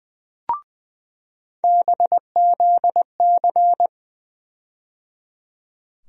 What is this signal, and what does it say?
Morse code, heard as a keyed tone, about 20 wpm, 700 Hz. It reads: BZC
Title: USA Call Sign Postfixes - 3 Letters 20wpm